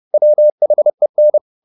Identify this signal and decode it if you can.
Morse code, heard as a keyed tone, about 30 wpm, 600 Hz. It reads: WHEN